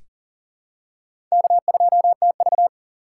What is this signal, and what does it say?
Morse code, heard as a keyed tone, about 40 wpm, 700 Hz. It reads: K2TV
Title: K1USN SST QSO Practice - 40wpm